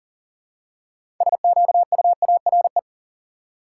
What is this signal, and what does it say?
Morse code, heard as a keyed tone, about 40 wpm, 700 Hz. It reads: SQUARE
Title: Six Letter Words - 40wpm